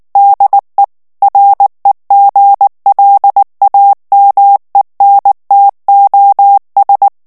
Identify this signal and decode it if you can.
Morse code, heard as a keyed tone, about 19 wpm, 800 Hz. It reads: DE REGLAMENTOS
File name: morse mp3 / ESPAGNOL TEXTE EN CLAIR 2 VITESSE 18 MOTS MINUTES